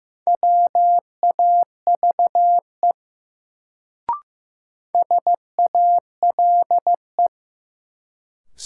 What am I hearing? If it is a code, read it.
Morse code, heard as a keyed tone, about 15 wpm, 700 Hz. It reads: WAVE SALE
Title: Sentences from Top 1400 Words - New Word Drill Alt - 15wpm